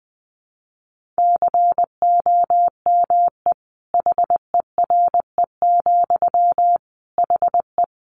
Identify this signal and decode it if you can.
Morse code, heard as a keyed tone, about 20 wpm, 700 Hz. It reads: COME HERE, HE
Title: Sentences from Top 1500 Words 20wpm